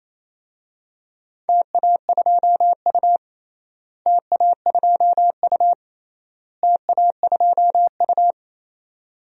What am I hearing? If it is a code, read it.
Morse code, heard as a keyed tone, about 28 wpm, 700 Hz. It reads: TA2U TA2U TA2U
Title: DXCC-100 Call Signs - 4 Characters - Repeated 3x - 28wpm